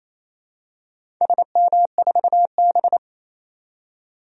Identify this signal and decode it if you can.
Morse code, heard as a keyed tone, about 28 wpm, 700 Hz. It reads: SM4B
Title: DXCC-100 Call Signs - 4 Characters - 28wpm